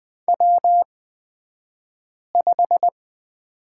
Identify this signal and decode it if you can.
Morse code, heard as a keyed tone, about 20 wpm, spaced as for 5 wpm, 700 Hz. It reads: W5
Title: 235 Lesson 38d - Letter q - Two Characters 20wpm